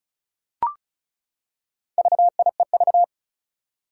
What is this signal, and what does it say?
Morse code, heard as a keyed tone, about 36 wpm, 700 Hz. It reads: VIEV
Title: Four Challenging Letter-Number Combinations 35wpm